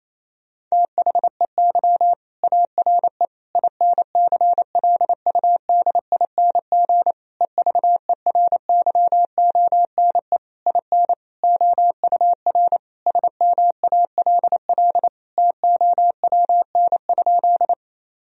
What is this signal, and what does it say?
Morse code, heard as a keyed tone, about 28 wpm, 700 Hz. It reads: THEY ARE INCLUDING EVERYONE IN OUR SMALL TOWN?
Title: Sentences from Top 1000 Words - Review Part 4 - 28wpm